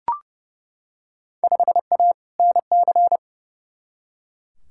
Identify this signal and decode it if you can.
Morse code, heard as a keyed tone, about 30 wpm, 700 Hz. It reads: 5A NC
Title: Field Day Exchanges - 30wpm